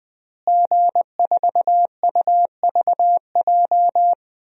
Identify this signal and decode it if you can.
Morse code, heard as a keyed tone, about 21 wpm, 700 Hz. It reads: G4UVJ
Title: DXCC-100 Call Signs - 5 Characters - Repeated 3x - 20wpm